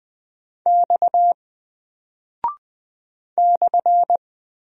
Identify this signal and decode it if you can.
Morse code, heard as a keyed tone, about 20 wpm, 700 Hz. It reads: X /
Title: Single Character (Letter:Number:Punctuation:Prosign) 20wpm